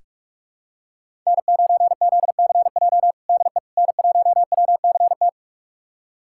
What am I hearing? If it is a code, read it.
Morse code, heard as a keyed tone, about 46 wpm, 700 Hz. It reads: N9ZCJ DE N1WCT
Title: USA Callsign DE Callsign - 5 Characters - 45wpm